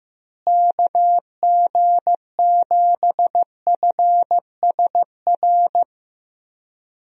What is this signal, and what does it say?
Morse code, heard as a keyed tone, about 15 wpm, 700 Hz. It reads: KG7FSR
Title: USA Call Signs - 6 Characters - Repeated 2x - 15wpm